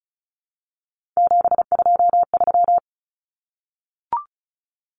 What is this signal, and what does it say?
Morse code, heard as a keyed tone, about 34 wpm, 700 Hz. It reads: 723
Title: Numbers - 3-Digit 35wpm